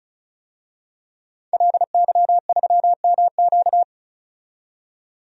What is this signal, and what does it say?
Morse code, heard as a keyed tone, about 36 wpm, 700 Hz. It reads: LY3MQ
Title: DXCC-100 Call Signs - 5 Characters - 35wpm